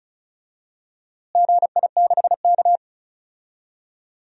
Morse code, heard as a keyed tone, about 34 wpm, 700 Hz. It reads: GI6K
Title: DXCC-100 Call Signs - 4 Characters - Repeated 2x - 35wpm